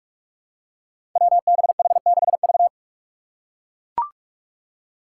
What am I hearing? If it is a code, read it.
Morse code, heard as a keyed tone, about 42 wpm, 700 Hz. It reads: WBH6V
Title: Five Challenging Letter-Number Combinations 45wpm